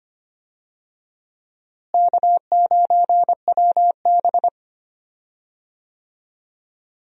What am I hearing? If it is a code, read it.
Morse code, heard as a keyed tone, about 25 wpm, 700 Hz. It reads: K9WB
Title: USA Call Signs - 4 Characters - Repeated 2x - 25wpm